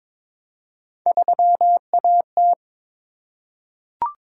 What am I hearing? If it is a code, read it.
Morse code, heard as a keyed tone, about 22 wpm, 700 Hz. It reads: 3AT
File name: practice / 22wpm / Three Letter-Number Combinations - 22wpm